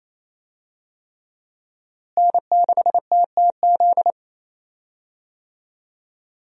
Morse code, heard as a keyed tone, about 27 wpm, 700 Hz. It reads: N6TTZ
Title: USA Call Signs - 5 Characters - Repeated 3x - 28wpm